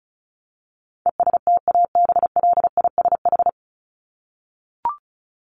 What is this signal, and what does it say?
Morse code, heard as a keyed tone, about 34 wpm, 700 Hz. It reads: ESTABLISH
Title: ED Suffix - Top 100 Words with Base Words 35wpm